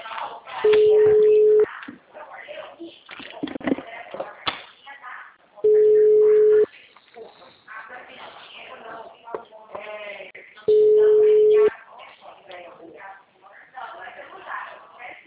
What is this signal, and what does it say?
Ringback tone (Europe, 425 Hz), with speech.
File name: out- -2001- - - .9949